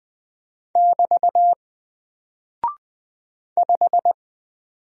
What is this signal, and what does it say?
Morse code, heard as a keyed tone, about 20 wpm, 700 Hz. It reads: <BT> 5
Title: Single Character (Letter:Number:Punctuation:Prosign) Alt 20wpm